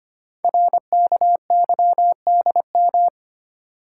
Morse code, heard as a keyed tone, about 25 wpm, 700 Hz. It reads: RKYDM
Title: Code Groups - 5 Letters - 25wpm